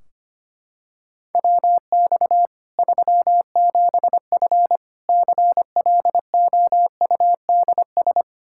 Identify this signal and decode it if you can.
Morse code, heard as a keyed tone, about 25 wpm, 700 Hz. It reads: WX 37F CLOUDS
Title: Practice QSO - Exchange 2 - Info - 25wpm